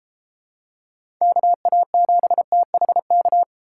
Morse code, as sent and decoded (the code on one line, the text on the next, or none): -.- .- --... - .... -.-
KA7THK